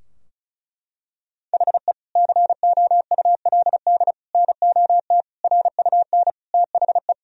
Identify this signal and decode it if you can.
Morse code, heard as a keyed tone, about 34 wpm, 700 Hz. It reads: HE COULD NOT RUN THE
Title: Sentences from Top 1000 Words - Review Part 2 - 35wpm